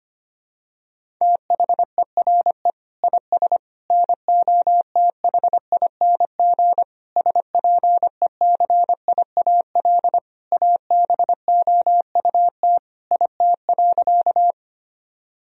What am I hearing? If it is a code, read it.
Morse code, heard as a keyed tone, about 25 wpm, 700 Hz. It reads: THERE IS NOTHING SPECIAL ABOUT IT.
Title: Sentences from Top 1000 Words - Review Part 3 - 25wpm